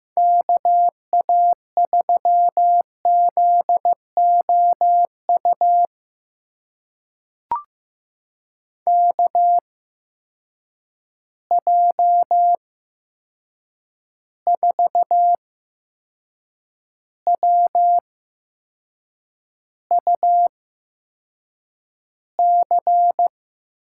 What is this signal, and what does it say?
Morse code, heard as a keyed tone, about 15 wpm, spaced as for 6 wpm, 700 Hz. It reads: KA3ZOU KJ4WUC
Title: USA Call Signs - 8x Character Spacing - 15wpm